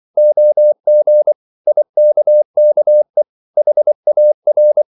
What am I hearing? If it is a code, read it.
Morse code, heard as a keyed tone, about 24 wpm, 600 Hz. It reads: OG IKKE HAR
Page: Ferie 24 wpm